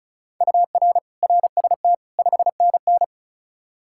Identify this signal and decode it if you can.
Morse code, heard as a keyed tone, about 36 wpm, 700 Hz. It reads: UR RST 5NN
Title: Practice QSO - Exchange 1 - Beginning - 35wpm